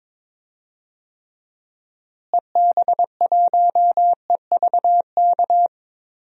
Morse code, heard as a keyed tone, about 22 wpm, 700 Hz. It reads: EB1EVK
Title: DXCC-100 Call Signs - 6 Characters - Repeated 2x - 22wpm